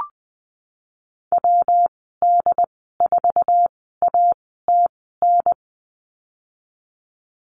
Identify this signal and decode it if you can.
Morse code, heard as a keyed tone, about 20 wpm, spaced as for 15 wpm, 700 Hz. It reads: WD4ATN
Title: 147 Lesson 24g - Letter y - Call Signs 20wpm